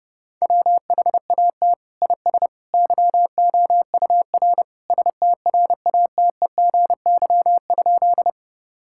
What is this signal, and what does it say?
Morse code, heard as a keyed tone, about 30 wpm, 700 Hz. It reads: WHAT IS YOUR STRATEGY?
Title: Sentences from Top 1000 Words - Review Part 5 - 30wpm